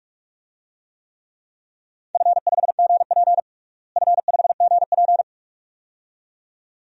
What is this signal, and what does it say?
Morse code, heard as a keyed tone, about 46 wpm, 700 Hz. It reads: F5GP F5GP
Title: DXCC-100 Call Signs - 4 Characters - Repeated 2x - 45wpm